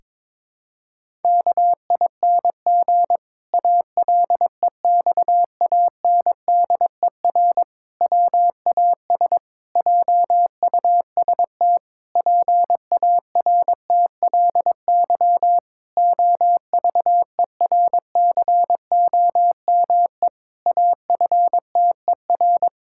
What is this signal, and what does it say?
Morse code, heard as a keyed tone, about 22 wpm, 700 Hz. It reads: KING ALEXANDER WAS JUST PARTLY OVERCOME AFTER
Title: Pangram Challenge v2 - 22wpm